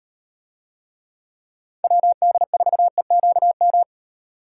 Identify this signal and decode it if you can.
Morse code, heard as a keyed tone, about 38 wpm, 700 Hz. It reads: WD4EQM